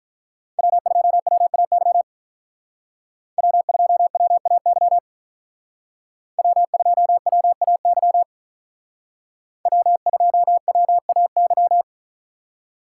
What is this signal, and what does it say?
Morse code, heard as a keyed tone, about 43 wpm, 700 Hz. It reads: W2WAY W2WAY W2WAY W2WAY
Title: Speed Racing - USA 1x3 Call Signs 35wpm